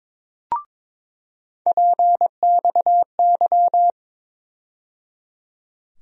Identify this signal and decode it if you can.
Morse code, heard as a keyed tone, about 22 wpm, 700 Hz. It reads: PXY
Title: USA Call Sign Postfixes - 3 Letters 22wpm